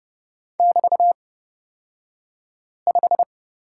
Morse code, heard as a keyed tone, about 30 wpm, spaced as for 5 wpm, 700 Hz. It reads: <BT>5
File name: Lesson 36d - BT - Two Characters 30wpm